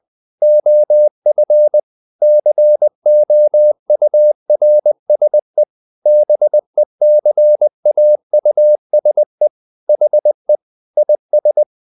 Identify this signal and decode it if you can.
Morse code, heard as a keyed tone, about 20 wpm, 600 Hz. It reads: OF COURSE BECAUSE HE IS
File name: the old man and the sea part 2 20wpm